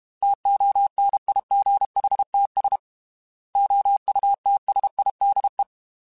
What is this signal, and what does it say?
Morse code, heard as a keyed tone, about 32 wpm, 800 Hz. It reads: TONIGHTS OUTSIDE
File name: Weather Sets / Weather Sets-mp3s / Weather 5 22 @30